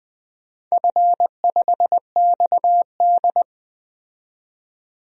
Morse code, heard as a keyed tone, about 20 wpm, 700 Hz. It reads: F5XD
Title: DXCC-100 Call Signs - 4 Characters - 20wpm